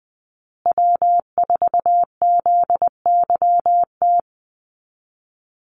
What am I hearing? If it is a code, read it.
Morse code, heard as a keyed tone, about 20 wpm, 700 Hz. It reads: W4ZYT
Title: Call Signs Repeated 3x - 20wpm